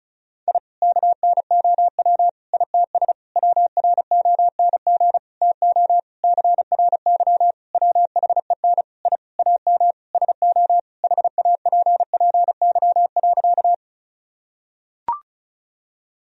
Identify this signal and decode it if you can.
Morse code, heard as a keyed tone, about 34 wpm, 700 Hz. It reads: I KNOW ITS WRONG TO CRY WHEN I AM SO HAPPY.
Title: Sentences from Top 1000 Words - Review Part 2 - 35wpm